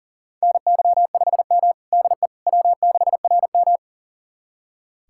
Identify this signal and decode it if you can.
Morse code, heard as a keyed tone, about 37 wpm, 700 Hz. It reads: NY5M DE W6RM
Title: USA Callsign DE Callsign - 4 Characters - 40wpm